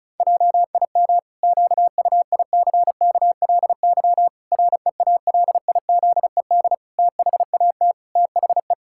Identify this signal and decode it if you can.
Morse code, heard as a keyed tone, about 36 wpm, 700 Hz. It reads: JIM QUICKLY REALIZED THAT THE
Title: Pangram Challenge v2 - 35wpm